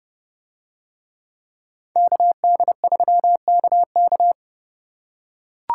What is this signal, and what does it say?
Morse code, heard as a keyed tone, about 28 wpm, 700 Hz. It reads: KD3KK